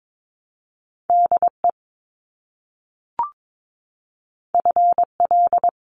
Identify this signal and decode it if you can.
Morse code, heard as a keyed tone, about 22 wpm, 700 Hz. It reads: DE FL
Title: US States Repeated 3x 22wpm